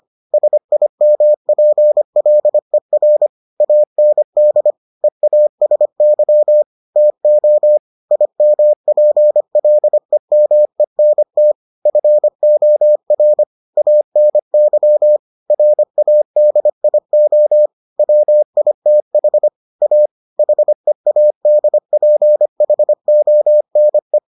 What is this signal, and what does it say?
Morse code, heard as a keyed tone, about 25 wpm, 600 Hz. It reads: SIMPLER AND EASY TO IMPLEMENT FOR ANY RADIO WITH A HEADPHONE